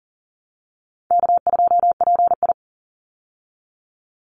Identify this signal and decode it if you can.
Morse code, heard as a keyed tone, about 40 wpm, 700 Hz. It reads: K2PI
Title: Easy Call Signs Repeated 3x - 40wpm